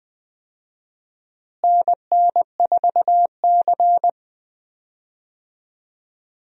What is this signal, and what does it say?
Morse code, heard as a keyed tone, about 21 wpm, 700 Hz. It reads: NN4C
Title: USA Call Signs - 4 Characters - Repeated 3x - 20wpm